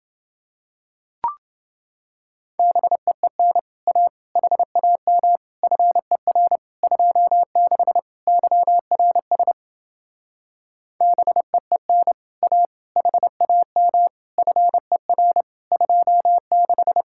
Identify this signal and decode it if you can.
Morse code, heard as a keyed tone, about 29 wpm, 700 Hz. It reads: BEEN A HAM FER 26 YRS BEEN A HAM FER 26